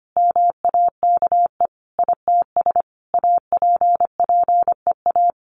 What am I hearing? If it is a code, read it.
Morse code, heard as a keyed tone, about 25 wpm, 700 Hz. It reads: MAKE ITS APPEA
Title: Sentences from Top 1100 Words 25wpm